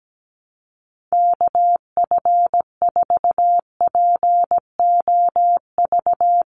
Morse code, heard as a keyed tone, about 17 wpm, 700 Hz. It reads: KF4POV
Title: USA Call Signs Repeated 3x 17wpm